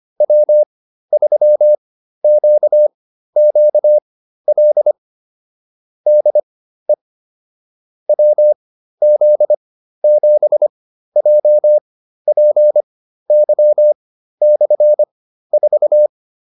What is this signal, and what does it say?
Morse code, heard as a keyed tone, about 25 wpm, spaced as for 13 wpm, 600 Hz. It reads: W3QQL DE WZ7JPY/4